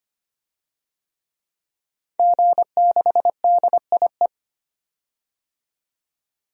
Morse code, heard as a keyed tone, about 25 wpm, 700 Hz. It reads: G6DIE
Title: DXCC-100 Call Signs - 5 Characters - 25wpm